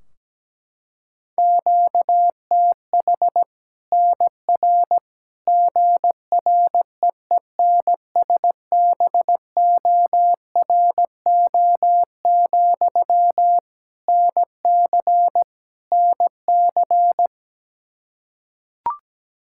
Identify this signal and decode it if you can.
Morse code, heard as a keyed tone, about 17 wpm, 700 Hz. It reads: QTH NR GREENSBORO, NC NC
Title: Practice QSO - Exchange 1 - Beginning - 17wpm